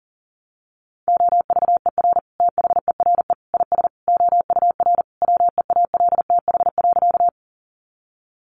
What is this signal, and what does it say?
Morse code, heard as a keyed tone, about 37 wpm, 700 Hz. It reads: OVER THERE IS OUR WEALTH.
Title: Sentences from Top 1200 Words 40wpm